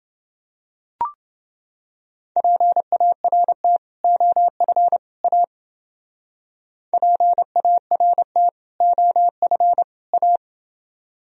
Morse code, heard as a keyed tone, about 29 wpm, 700 Hz. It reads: PART OF A PART OF A